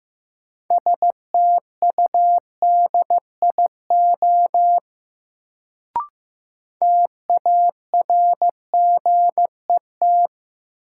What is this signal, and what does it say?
Morse code, heard as a keyed tone, about 15 wpm, 700 Hz. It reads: STUDIO TARGET
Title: Sentences from Top 1200 Words - New Word Drill - 15wpm